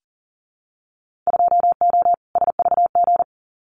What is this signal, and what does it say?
Morse code, heard as a keyed tone, about 40 wpm, 700 Hz. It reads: 2O SVG
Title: Winter Field Day Exchanges 40wpm